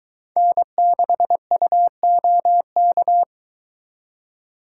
Morse code, heard as a keyed tone, about 22 wpm, 700 Hz. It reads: N6UOK